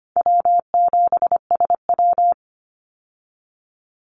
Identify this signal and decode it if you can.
Morse code, heard as a keyed tone, about 25 wpm, 700 Hz. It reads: W7SW
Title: Call Signs Repeated 2x - 25wpm